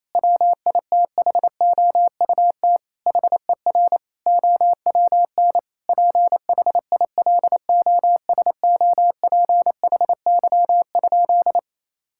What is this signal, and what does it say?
Morse code, heard as a keyed tone, about 28 wpm, 700 Hz. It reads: WITHOUT HER OWN PHILOSOPHY?
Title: Sentences from Top 1000 Words - Review Part 6 - 28wpm